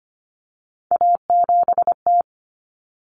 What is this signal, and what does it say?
Morse code, heard as a keyed tone, about 25 wpm, 700 Hz. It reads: A7T
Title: Common Cut Numbers - 3-Digit 25wpm